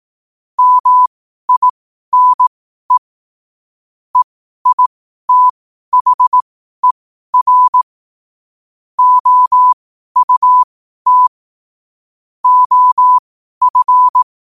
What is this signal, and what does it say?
Morse code, heard as a keyed tone, about 18 wpm, spaced as for 12 wpm, 1000 Hz. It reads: MINE EITHER OUT OF
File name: qotdfiles / 12wpm